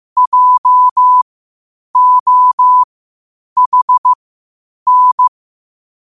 Morse code, heard as a keyed tone, about 15 wpm, spaced as for 8 wpm, 1000 Hz. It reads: JOHN
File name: qotdfiles / 8wpm